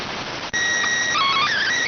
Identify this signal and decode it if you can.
SSTV (slow-scan television), PD 290.